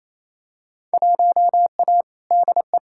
Morse code, heard as a keyed tone, about 28 wpm, 700 Hz. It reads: 1A DE
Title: Field Day Exchanges Encore - 28wpm